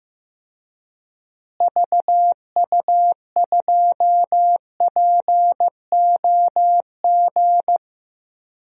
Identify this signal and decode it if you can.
Morse code, heard as a keyed tone, about 15 wpm, 700 Hz. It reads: VU2POG
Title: DXCC-100 Call Signs - 6 Characters - Repeated 3x - 15wpm